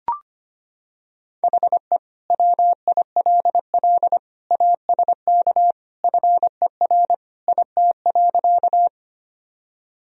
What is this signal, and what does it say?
Morse code, heard as a keyed tone, about 25 wpm, 700 Hz. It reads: HE WILL ASK FER IT.
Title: Sentences from Top 200 Words and Abbreviations 25wpm